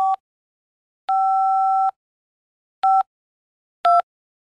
DTMF, heard as telephone touch tones, tones of about 190 ms. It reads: 5 5 2